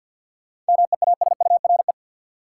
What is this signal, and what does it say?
Morse code, heard as a keyed tone, about 51 wpm, 700 Hz. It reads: MEASURE
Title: Sentences from Top 1000 Words - Review Part 1 - New Word Drill - 50wpm